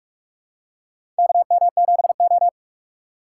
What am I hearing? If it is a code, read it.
Morse code, heard as a keyed tone, about 46 wpm, 700 Hz. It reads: KM7O
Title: USA Call Signs - 4 Characters - 3x Character Spacing - 45wpm